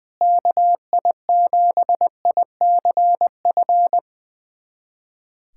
Morse code, heard as a keyed tone, about 20 wpm, 700 Hz. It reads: KI7ICF
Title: USA Callsign DE Callsign - 6 Characters - 20wpm